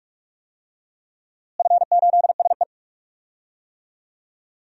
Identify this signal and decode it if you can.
Morse code, heard as a keyed tone, about 46 wpm, 700 Hz. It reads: F8SE
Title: DXCC-100 Call Signs - 4 Characters - Repeated 2x - 45wpm